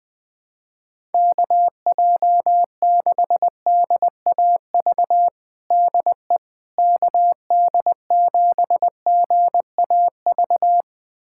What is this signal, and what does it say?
Morse code, heard as a keyed tone, about 21 wpm, 700 Hz. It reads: KJ6DAV DE KD7GAV
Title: USA Callsign DE Callsign - 6 Characters - 20wpm